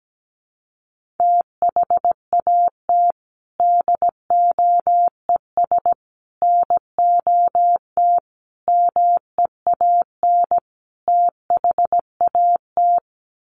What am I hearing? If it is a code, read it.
Morse code, heard as a keyed tone, about 17 wpm, 700 Hz. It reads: THAT DOES NOT MEAN THAT
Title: Sets of 5 Words Encore 17wpm